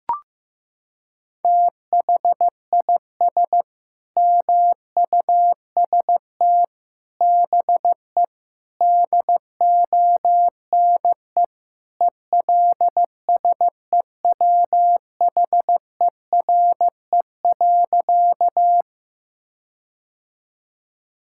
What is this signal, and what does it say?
Morse code, heard as a keyed tone, about 15 wpm, 700 Hz. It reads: THIS MUST BE DONE ELSEWHERE.
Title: Sentences from Top 1600 Words 15wpm